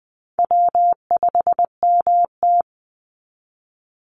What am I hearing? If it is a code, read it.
Morse code, heard as a keyed tone, about 20 wpm, 700 Hz. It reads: W5MT
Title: Easy Call Signs Repeated 3x - 20wpm